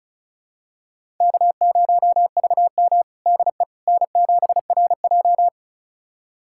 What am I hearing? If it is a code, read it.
Morse code, heard as a keyed tone, about 36 wpm, 700 Hz. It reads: K0VM DE N7RJ
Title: USA Callsign DE Callsign - 4 Characters - 35wpm